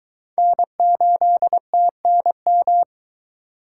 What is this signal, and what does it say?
Morse code, heard as a keyed tone, about 21 wpm, 700 Hz. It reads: N8TNM